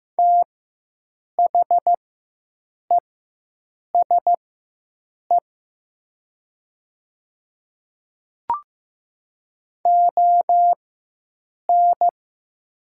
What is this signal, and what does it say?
Morse code, heard as a keyed tone, about 15 wpm, spaced as for 7 wpm, 700 Hz. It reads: THESE ON
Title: 4x Character Spacing - Top 200 Words - 15wpm